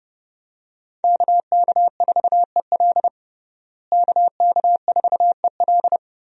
Morse code, heard as a keyed tone, about 30 wpm, 700 Hz. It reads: KK4EL KK4EL
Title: USA Call Signs - 5 Characters - Repeated 3x - 30wpm